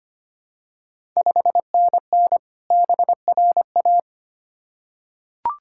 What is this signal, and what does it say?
Morse code, heard as a keyed tone, about 25 wpm, 700 Hz. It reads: 5NN BRA
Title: Ontario QSO Party Exchange 25wpm